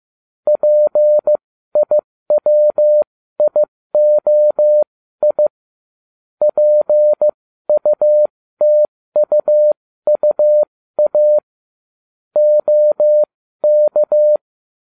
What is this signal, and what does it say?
Morse code, heard as a keyed tone, about 15 wpm, spaced as for 13 wpm, 600 Hz. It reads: PIWIOI PUTUUA OK